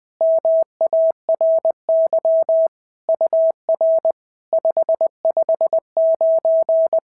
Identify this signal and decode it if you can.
Morse code, heard as a keyed tone, about 20 wpm, 690 Hz. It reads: MARY UR 559
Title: SOTA QSO Practice 20wpm